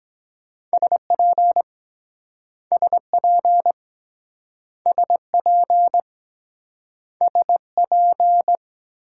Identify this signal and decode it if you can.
Morse code, heard as a keyed tone, about 23 wpm, 700 Hz. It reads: SP SP SP SP